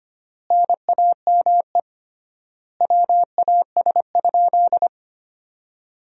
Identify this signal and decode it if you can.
Morse code, heard as a keyed tone, about 25 wpm, 700 Hz. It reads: NAME WAS?
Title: 3x Word-Spacing - Sentences from Top 200 Words - 25wpm